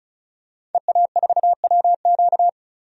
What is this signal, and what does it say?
Morse code, heard as a keyed tone, about 36 wpm, 700 Hz. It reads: EA4WQ
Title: DXCC-100 Call Signs - 5 Characters - 35wpm